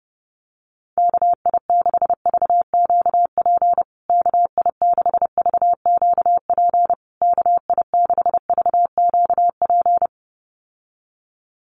Morse code, heard as a keyed tone, about 30 wpm, 700 Hz. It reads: KI6VQP KI6VQP KI6VQP
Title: USA Call Signs Repeated 3x 30wpm